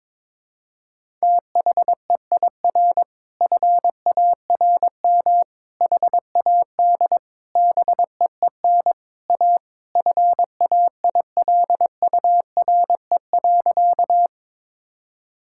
Morse code, heard as a keyed tone, about 22 wpm, 700 Hz. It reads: THEIR FARM HAD BEEN A FAILURE.
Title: Sentences from Top 1000 Words - Review Part 6 - 22wpm